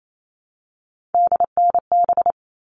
Morse code, heard as a keyed tone, about 28 wpm, 700 Hz. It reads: DNB
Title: All Cut Numbers - 3-Digit 28wpm